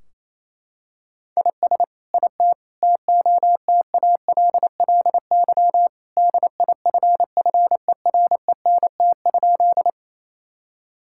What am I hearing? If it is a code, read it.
Morse code, heard as a keyed tone, about 28 wpm, 700 Hz. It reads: IS IT TOTALLY DIFFERENT?
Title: Sentences from Top 1600 Words 28wpm